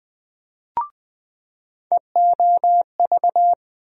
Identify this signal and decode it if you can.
Morse code, heard as a keyed tone, about 20 wpm, 700 Hz. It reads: EOV
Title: USA Call Sign Postfixes - 3 Letters 20wpm